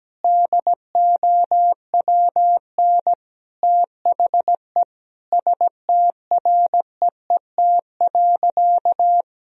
Morse code, heard as a keyed tone, about 17 wpm, 700 Hz. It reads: DOWN THE STREET.
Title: Sentences from Top 1000 Words - Review Part 3 - 17wpm